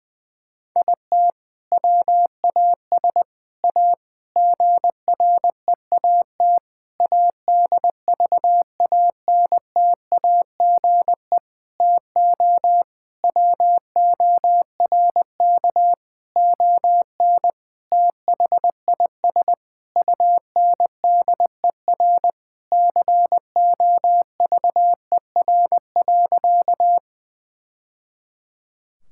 Morse code, heard as a keyed tone, about 20 wpm, 700 Hz. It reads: IT WAS A GREAT ADVANTAGE TO WORK ON THIS UNDER COVER.
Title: Sentences from Top 1000 Words - Review Part 6 - 20wpm